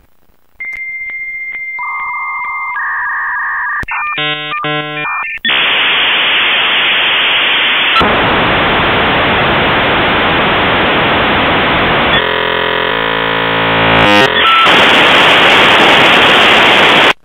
Dial-up modem handshake.